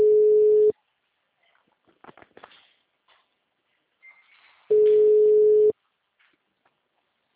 Ringback tone (Europe, 425 Hz).
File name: out- -2001- - - .2039